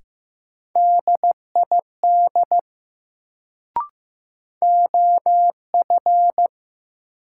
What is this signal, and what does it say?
Morse code, heard as a keyed tone, about 15 wpm, 700 Hz. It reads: DID OF
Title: Sentences from Top 100 Words - New Word Drill - 15wpm